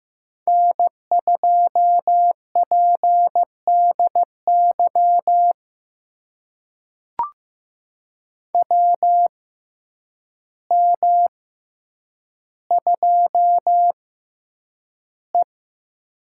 Morse code, heard as a keyed tone, about 15 wpm, spaced as for 8 wpm, 700 Hz. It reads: N2PDY WM2E
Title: USA Call Signs - 6x Character Spacing - 15wpm